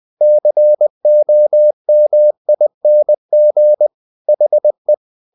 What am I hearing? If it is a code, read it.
Morse code, heard as a keyed tone, about 20 wpm, 600 Hz. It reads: COMING HE
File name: the old man and the sea part 2 20wpm